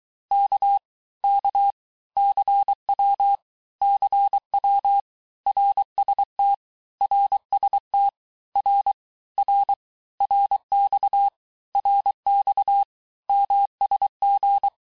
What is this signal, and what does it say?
Morse code, heard as a keyed tone, about 23 wpm, 800 Hz. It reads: K K CW CW RST RST R R RX RX MSG
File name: teste abreviations vitesse 17mpm puis 22mpm